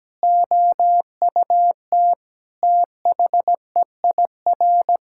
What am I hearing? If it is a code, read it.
Morse code, heard as a keyed tone, about 17 wpm, 700 Hz. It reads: OUT THEIR
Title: Sentences from Top 1000 Words - Review Part 6 - 17wpm